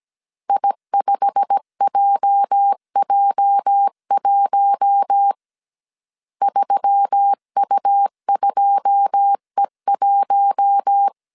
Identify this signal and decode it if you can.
Morse code, heard as a keyed tone, about 17 wpm, 800 Hz. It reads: I5JJ1 3U2E1
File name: CW / DET / 1 9 1500C S C 16wpm 2909